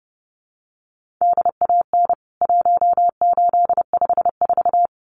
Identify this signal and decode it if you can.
Morse code, heard as a keyed tone, about 30 wpm, 700 Hz. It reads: DAN 1854
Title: CWT Exchange 30wpm